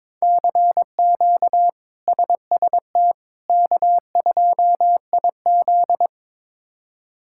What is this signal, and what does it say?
Morse code, heard as a keyed tone, about 22 wpm, 700 Hz. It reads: CQ SST K2IZ
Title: K1USN SST QSO Practice - 22wpm